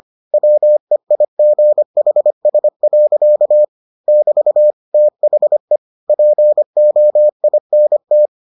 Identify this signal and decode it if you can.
Morse code, heard as a keyed tone, about 25 wpm, 600 Hz. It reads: WEIGHS. <BT> THE POINT